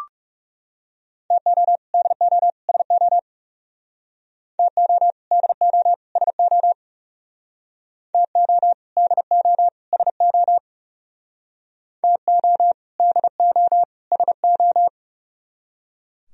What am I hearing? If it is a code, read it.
Morse code, heard as a keyed tone, about 38 wpm, 700 Hz. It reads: TO DO SO TO DO SO TO DO SO TO DO SO